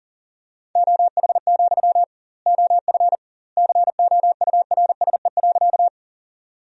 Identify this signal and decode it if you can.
Morse code, heard as a keyed tone, about 40 wpm, 700 Hz. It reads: OH, OF COURSE.
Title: 2x Word-Spacing - Sentences from Top 500 Words - 40wpm